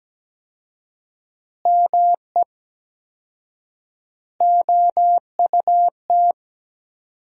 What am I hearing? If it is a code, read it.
Morse code, heard as a keyed tone, about 17 wpm, 700 Hz. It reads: ME OUT
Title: 4x Word-Spacing - Sentences from Top 400 Words - 17wpm